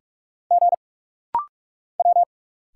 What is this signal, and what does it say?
Morse code, heard as a keyed tone, about 46 wpm, 700 Hz. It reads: G W
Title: Single Letters - Rapid-Fire 45wpm